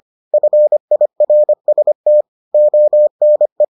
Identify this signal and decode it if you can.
Morse code, heard as a keyed tone, about 25 wpm, 600 Hz. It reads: FIRST ONE